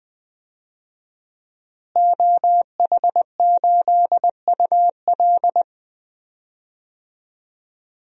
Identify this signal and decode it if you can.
Morse code, heard as a keyed tone, about 20 wpm, 700 Hz. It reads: OH8UL
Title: DXCC-100 Call Signs - 5 Characters - Repeated 2x - 20wpm